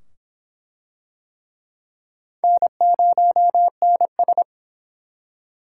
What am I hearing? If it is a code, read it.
Morse code, heard as a keyed tone, about 24 wpm, 700 Hz. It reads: N0NS